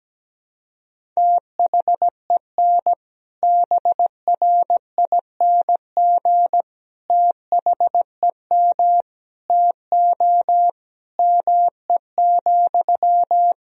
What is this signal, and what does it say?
Morse code, heard as a keyed tone, about 17 wpm, 700 Hz. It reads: THEN BRING THEM TO ME,
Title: Sentences from Top 500 Words and Abbreviations 17wpm